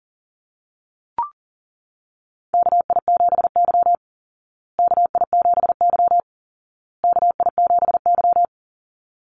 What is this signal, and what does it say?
Morse code, heard as a keyed tone, about 40 wpm, 700 Hz. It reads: KI7Y KI7Y KI7Y
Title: Call Signs Repeated 2x - 40wpm